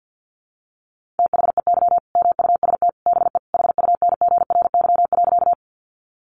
Morse code, heard as a keyed tone, about 50 wpm, 700 Hz. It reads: THEY MUST BE HUNGRY.
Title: Sentences from Top 1400 Words 50wpm